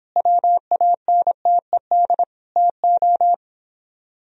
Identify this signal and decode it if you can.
Morse code, heard as a keyed tone, about 29 wpm, 700 Hz. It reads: WANTED TO